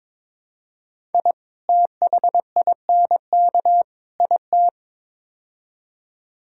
Morse code, heard as a keyed tone, about 22 wpm, 700 Hz. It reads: I THINK IT
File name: practice / 22wpm / Build A Phrase - 5-Word Phrases - 22wpm